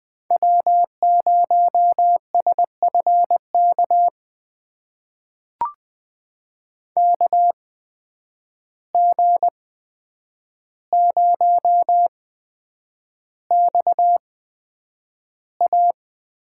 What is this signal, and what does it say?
Morse code, heard as a keyed tone, about 20 wpm, spaced as for 9 wpm, 700 Hz. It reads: W0SFK KG0XA
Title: USA Call Signs - 5 Characters - 8x Character Spacing - 20wpm